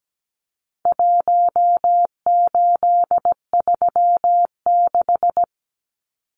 Morse code, heard as a keyed tone, about 17 wpm, 700 Hz. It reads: 1836
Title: Numbers - Contest Style - 4-Digit 17wpm